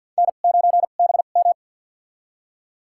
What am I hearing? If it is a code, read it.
Morse code, heard as a keyed tone, about 53 wpm, spaced as for 37 wpm, 700 Hz. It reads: N9BM
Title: USA Call Signs - 4 Characters - 2x Character Spacing - 50wpm